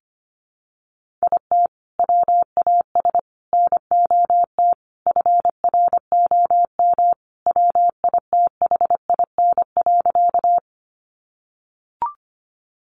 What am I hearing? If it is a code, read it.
Morse code, heard as a keyed tone, about 25 wpm, 700 Hz. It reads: IT WAS NOT FROM WITHIN.
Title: Sentences from Top 700 Words 25wpm